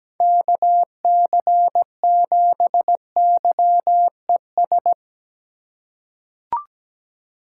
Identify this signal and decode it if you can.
Morse code, heard as a keyed tone, about 17 wpm, 700 Hz. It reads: KC7YES
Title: USA Call Signs - 6 Characters - 3x Character Spacing - 17wpm